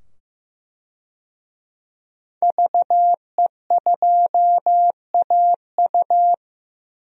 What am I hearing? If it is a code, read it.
Morse code, heard as a keyed tone, about 15 wpm, 700 Hz. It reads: VE2AU
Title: DXCC-100 Call Signs - 5 Characters - 15wpm